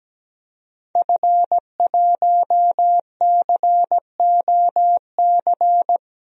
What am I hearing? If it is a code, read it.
Morse code, heard as a keyed tone, about 17 wpm, 700 Hz. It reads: F1COC
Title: DXCC-100 Call Signs - 5 Characters - Repeated 2x - 17wpm